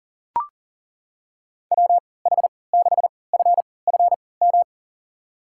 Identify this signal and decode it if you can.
Morse code, heard as a keyed tone, about 40 wpm, spaced as for 23 wpm, 700 Hz. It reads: WH6FFM
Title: USA Call Signs - 6 Characters - 3x Character Spacing - 40wpm